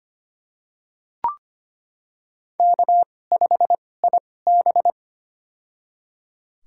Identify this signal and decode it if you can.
Morse code, heard as a keyed tone, about 25 wpm, spaced as for 18 wpm, 700 Hz. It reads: K5IB
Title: USA Call Signs - 4 Characters - 2x Character Spacing - 25wpm